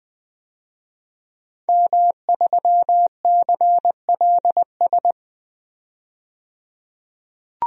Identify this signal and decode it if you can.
Morse code, heard as a keyed tone, about 20 wpm, 700 Hz. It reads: M3CLS
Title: DXCC-100 Call Signs - 5 Characters - 20wpm